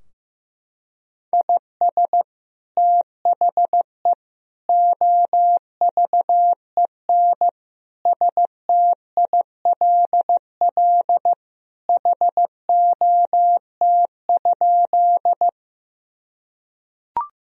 Morse code, heard as a keyed tone, about 15 wpm, 700 Hz. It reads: IS THE OVEN STILL HOT?
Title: Sentences from Top 1000 Words - Review Part 4 - 15wpm